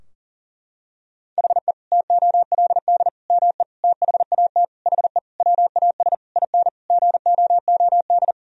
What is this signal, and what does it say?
Morse code, heard as a keyed tone, about 40 wpm, 700 Hz. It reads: HE TOLD ME THAT HE WAS IN GOOD
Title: Sentences from Top 1000 Words - Review Part 3 - 40wpm